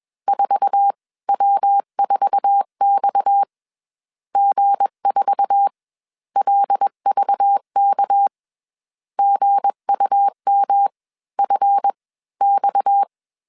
Morse code, heard as a keyed tone, about 21 wpm, spaced as for 13 wpm, 800 Hz. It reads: 4W4XG 4LVKG UMFX